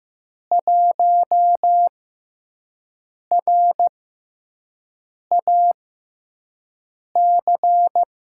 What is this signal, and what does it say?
Morse code, heard as a keyed tone, about 15 wpm, spaced as for 5 wpm, 700 Hz. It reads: 1RAC
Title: USA Call Signs - 6x Character Spacing - 15wpm